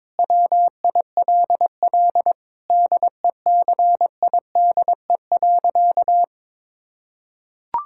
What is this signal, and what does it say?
Morse code, heard as a keyed tone, about 21 wpm, 700 Hz. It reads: WILL DECIDE.
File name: Sentences with 7-Word Phrases - 22wpm